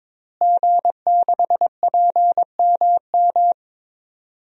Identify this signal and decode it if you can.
Morse code, heard as a keyed tone, about 22 wpm, 700 Hz. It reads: G6PMM
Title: DXCC-100 Call Signs - 5 Characters - Repeated 2x - 22wpm